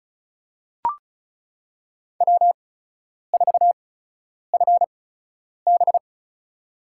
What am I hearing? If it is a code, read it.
Morse code, heard as a keyed tone, about 34 wpm, spaced as for 10 wpm, 700 Hz. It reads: W4FB
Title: USA Call Signs - 4 Characters - 8x Character Spacing - 35wpm